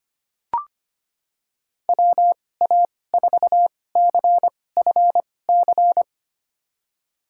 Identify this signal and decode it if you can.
Morse code, heard as a keyed tone, about 25 wpm, spaced as for 18 wpm, 700 Hz. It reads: WA4CFC
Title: USA Call Signs - 6 Characters - 2x Character Spacing - 25wpm